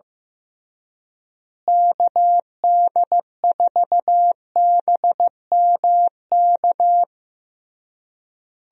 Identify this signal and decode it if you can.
Morse code, heard as a keyed tone, about 15 wpm, 700 Hz. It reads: KD4BMK
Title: USA Call Signs - 6 Characters - Repeated 3x - 15wpm